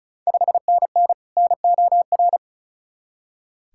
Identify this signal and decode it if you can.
Morse code, heard as a keyed tone, about 36 wpm, 690 Hz. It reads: 5NN NOR
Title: Ontario QSO Party Exchange 35wpm